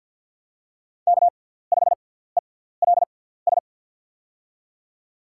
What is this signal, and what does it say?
Morse code, heard as a keyed tone, about 47 wpm, spaced as for 17 wpm, 700 Hz. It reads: K5ELS
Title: USA Call Signs - 5 Characters - 6x Character Spacing - 50wpm